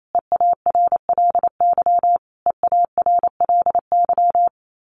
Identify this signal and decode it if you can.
Morse code, heard as a keyed tone, about 28 wpm, 700 Hz. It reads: EARLY EARLY
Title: Top 500 Words Repeated 28wpm